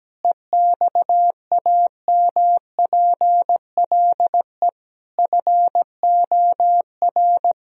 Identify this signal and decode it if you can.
Morse code, heard as a keyed tone, about 17 wpm, 700 Hz. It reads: EXAMPLE FOR
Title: Sentences from Top 1000 Words - Review Part 1 - 17wpm